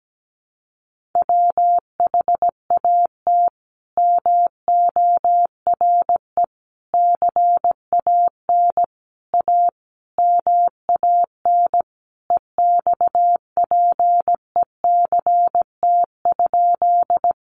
Morse code, heard as a keyed tone, about 17 wpm, 700 Hz. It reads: WHAT MORE CAN A MAN EXPECT?
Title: Sentences from Top 1100 Words 17wpm